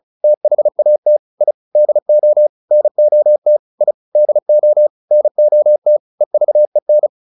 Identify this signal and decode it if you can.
Morse code, heard as a keyed tone, about 37 wpm, 600 Hz. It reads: THAT I DO NOT I DO NOT EVEN